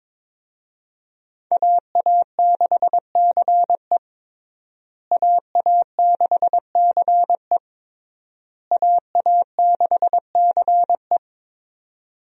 Morse code, heard as a keyed tone, about 22 wpm, 700 Hz. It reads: AA6CE AA6CE AA6CE
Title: USA Call Signs - 5 Characters - Repeated 3x - 22wpm